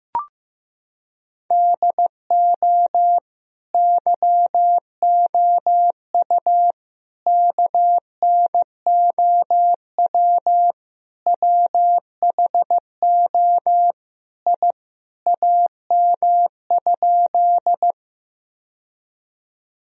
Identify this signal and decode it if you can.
Morse code, heard as a keyed tone, about 15 wpm, 700 Hz. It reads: DO YOU KNOW WHO I AM?
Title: Sentences from Top 1000 Words - Review Part 2 - 15wpm